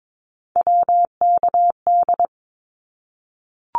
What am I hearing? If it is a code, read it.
Morse code, heard as a keyed tone, about 22 wpm, 700 Hz. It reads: WKD
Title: Abbreviations 22wpm